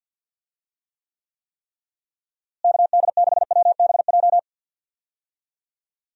Morse code, heard as a keyed tone, about 52 wpm, 700 Hz. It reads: KD6WBJ
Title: USA Call Signs - 6 Characters - Repeated 3x - 50wpm